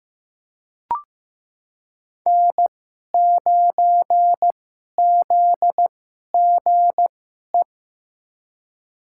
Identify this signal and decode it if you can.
Morse code, heard as a keyed tone, about 15 wpm, spaced as for 11 wpm, 700 Hz. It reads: N9ZGE
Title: USA Call Signs - 5 Characters - 2x Character Spacing - 15wpm